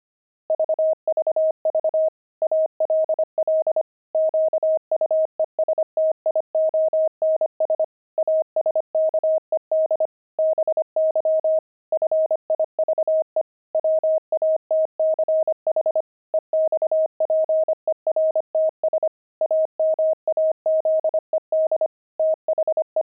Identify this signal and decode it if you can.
Morse code, heard as a keyed tone, about 25 wpm, 630 Hz. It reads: VVV ALL QUESTIONS ASKED BY FIVE WATCH EXPERTS AMAZED THE
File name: pangram-fs-25-cs-25-watches